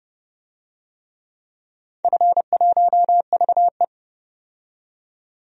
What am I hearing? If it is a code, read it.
Morse code, heard as a keyed tone, about 31 wpm, 700 Hz. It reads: F1VE
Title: DXCC-100 Call Signs - 4 Characters - 30wpm